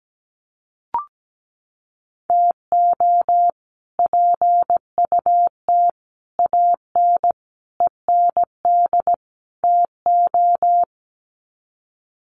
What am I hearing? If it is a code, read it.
Morse code, heard as a keyed tone, about 17 wpm, 700 Hz. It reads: TO PUT AN END TO
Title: Sets of 5 Words Encore2 17wpm